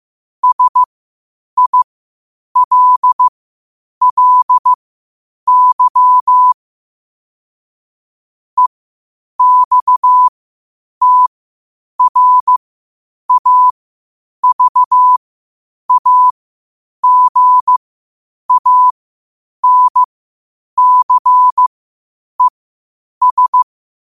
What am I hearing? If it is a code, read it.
Morse code, heard as a keyed tone, about 15 wpm, spaced as for 8 wpm, 1000 Hz. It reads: SILLY EXTRAVAGANCES
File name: qotdfiles / 8wpm